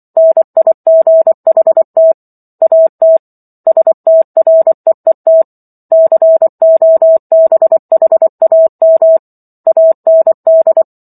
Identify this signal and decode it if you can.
Morse code, heard as a keyed tone, about 24 wpm, 650 Hz. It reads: NIGHT AT STREET COBHAM AND